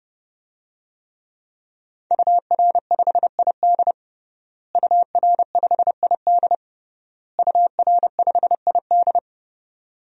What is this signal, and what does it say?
Morse code, heard as a keyed tone, about 32 wpm, 700 Hz. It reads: UR5ID UR5ID UR5ID
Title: DXCC-100 Call Signs - 5 Characters - Repeated 3x - 30wpm